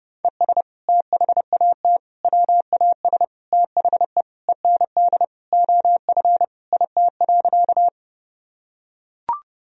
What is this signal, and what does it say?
Morse code, heard as a keyed tone, about 32 wpm, 700 Hz. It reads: ES THAT WAS THE END OF IT.
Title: Sentences from Top 200 Words and Abbreviations 30wpm